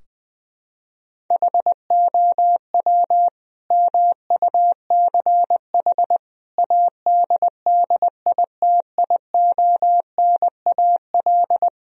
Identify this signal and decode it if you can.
Morse code, heard as a keyed tone, about 20 wpm, 700 Hz. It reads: HOW MUCH ADDITIONAL
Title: Sentences from Top 1000 Words - Review Part 4 - 20wpm